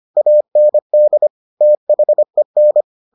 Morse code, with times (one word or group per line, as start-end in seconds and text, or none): 0.2-1.3 AND
1.6-2.8 THEN